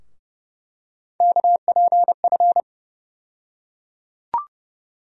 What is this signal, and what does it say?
Morse code, heard as a keyed tone, about 30 wpm, 700 Hz. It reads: KPF
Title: USA Call Sign Postfixes - 3 Letters 30wpm